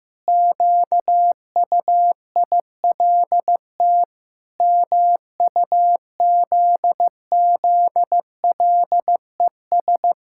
Morse code, heard as a keyed tone, about 16 wpm, 700 Hz. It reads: QUILT MUZZLES
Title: Pangram Challenge v2 - 15wpm